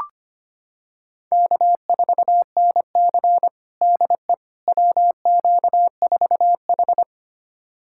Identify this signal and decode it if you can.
Morse code, heard as a keyed tone, about 25 wpm, 700 Hz. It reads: K4NC DE WQ4H
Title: USA Callsign DE Callsign - 4 Characters - 25wpm